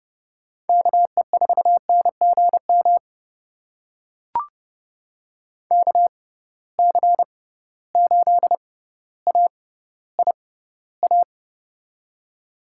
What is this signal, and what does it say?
Morse code, heard as a keyed tone, about 30 wpm, spaced as for 15 wpm, 700 Hz. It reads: KE4NGM KC8AIA
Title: USA Call Signs - 6 Characters - 6x Character Spacing - 30wpm